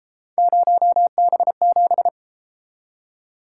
Morse code, heard as a keyed tone, about 32 wpm, 700 Hz. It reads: 0B7